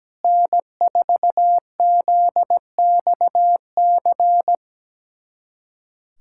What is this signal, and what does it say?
Morse code, heard as a keyed tone, about 17 wpm, 700 Hz. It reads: N4ZXC
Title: USA Callsign DE Callsign - 5 Characters - 17wpm